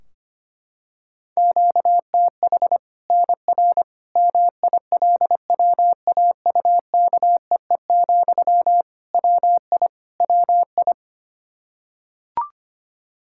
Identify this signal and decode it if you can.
Morse code, heard as a keyed tone, about 25 wpm, 700 Hz. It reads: QTH NR MILWAUKEE, WI WI
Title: K1USN SST QSO Practice 25wpm